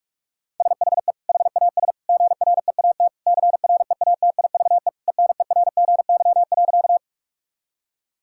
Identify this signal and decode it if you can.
Morse code, heard as a keyed tone, about 46 wpm, 700 Hz. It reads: SHE HAS GREAT CREATIVE ENERGY.
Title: Sentences from Top 1000 Words - Review Part 6 - 45wpm